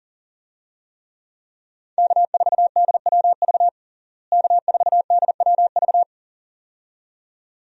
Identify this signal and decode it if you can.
Morse code, heard as a keyed tone, about 40 wpm, 700 Hz. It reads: K4DWV K4DWV
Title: USA Call Signs - 5 Characters - Repeated 2x - 40wpm